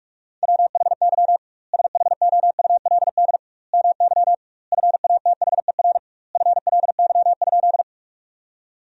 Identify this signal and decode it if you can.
Morse code, heard as a keyed tone, about 41 wpm, 700 Hz. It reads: WHY SHOULD MY FATHER FLY?
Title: 2x Word-Spacing - Sentences from Top 500 Words - 45wpm